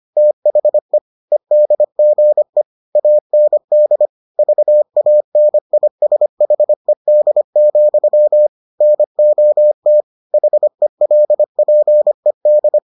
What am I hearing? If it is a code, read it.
Morse code, heard as a keyed tone, about 25 wpm, 600 Hz. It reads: THE EDGE AND VANISHED, NOT HELPED